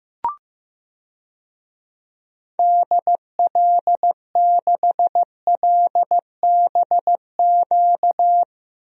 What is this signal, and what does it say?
Morse code, heard as a keyed tone, about 15 wpm, 700 Hz. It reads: DL6LBQ
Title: DXCC-100 Call Signs - 6 Characters - Repeated 3x - 15wpm